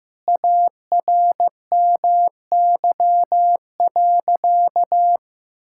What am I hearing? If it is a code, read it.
Morse code, heard as a keyed tone, about 15 wpm, 700 Hz. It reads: ARMY.
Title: Sentences from Top 1000 Words - Review Part 6 - 15wpm